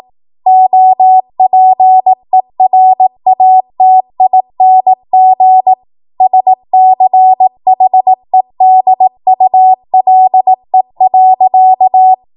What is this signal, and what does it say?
Morse code, heard as a keyed tone, about 18 wpm, 750 Hz. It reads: OPERATING SCHEDULE.
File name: Archive / 18 WPM / 18WPM